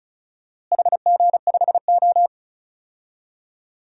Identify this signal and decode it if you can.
Morse code, heard as a keyed tone, about 34 wpm, 700 Hz. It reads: HG5O
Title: DXCC-100 Call Signs - 4 Characters - Repeated 2x - 35wpm